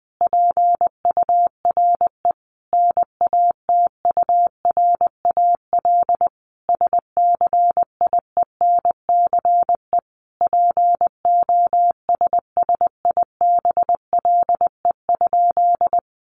Morse code, heard as a keyed tone, about 20 wpm, 700 Hz. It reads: PURE NATURAL SCIENCE POSSIBLE?
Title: Sentences from Top 1500 Words 20wpm